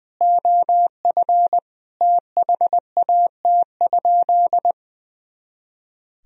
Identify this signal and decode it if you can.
Morse code, heard as a keyed tone, about 20 wpm, 700 Hz. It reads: OF THAT?
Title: Sentences from Top 100 Words and Abbreviations 20wpm